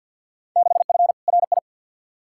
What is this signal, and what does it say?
Morse code, heard as a keyed tone, about 50 wpm, 700 Hz. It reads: 6F RI
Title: Field Day Exchanges Encore - 50wpm